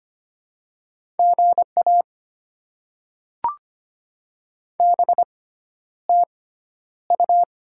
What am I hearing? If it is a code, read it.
Morse code, heard as a keyed tone, about 25 wpm, spaced as for 11 wpm, 700 Hz. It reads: GA BTU
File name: Top 20 QSO Elements - 6x Character Spacing - 25wpm